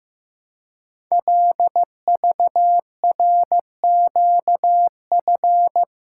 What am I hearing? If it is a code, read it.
Morse code, heard as a keyed tone, about 15 wpm, 700 Hz. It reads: LVRQF
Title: Code Groups - 5 Letters - 15wpm